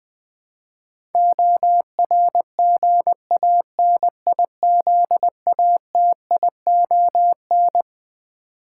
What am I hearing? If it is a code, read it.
Morse code, heard as a keyed tone, about 20 wpm, 700 Hz. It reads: ORGANIZATION
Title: Twelve Letter Words - 20wpm